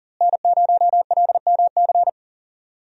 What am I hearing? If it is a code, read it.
Morse code, heard as a keyed tone, about 40 wpm, 700 Hz. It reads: N0LMC
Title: USA Call Signs - 5 Characters - Repeated 3x - 40wpm